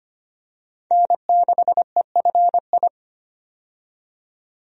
Morse code, heard as a keyed tone, about 25 wpm, 700 Hz. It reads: N6EFI
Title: USA Call Signs - 5 Characters - 25wpm